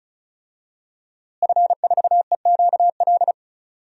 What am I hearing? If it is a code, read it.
Morse code, heard as a keyed tone, about 36 wpm, 690 Hz. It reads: F4EQL
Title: DXCC-100 Call Signs - 5 Characters - Repeated 3x - 35wpm